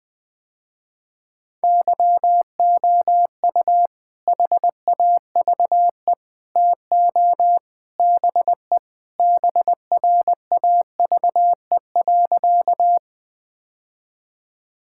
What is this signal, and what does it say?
Morse code, heard as a keyed tone, about 20 wpm, 700 Hz. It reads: YOU HAVE TO BE BRAVE.
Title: Sentences with 3-Word Phrases - Spoken Beginning - 20wpm